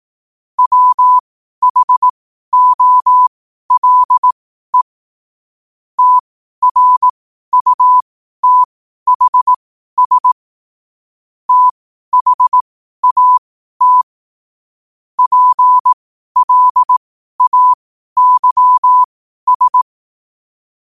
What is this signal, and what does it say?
Morse code, heard as a keyed tone, about 18 wpm, spaced as for 12 wpm, 1000 Hz. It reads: WHOLE TRUTHS THAT PLAYS
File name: qotdfiles / 12wpm